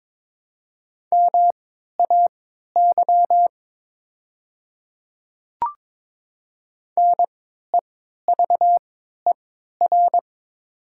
Morse code, heard as a keyed tone, about 22 wpm, spaced as for 13 wpm, 690 Hz. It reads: MAY NEVER
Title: 3x Character Spacing - Top 300 Words - 22wpm